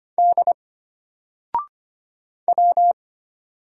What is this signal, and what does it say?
Morse code, heard as a keyed tone, about 25 wpm, 700 Hz. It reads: D W
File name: Single-Letter-Number-25wpm